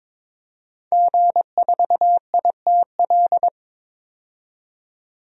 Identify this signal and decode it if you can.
Morse code, heard as a keyed tone, about 22 wpm, 700 Hz. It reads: G4ITL
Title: DXCC-100 Call Signs - 5 Characters - 22wpm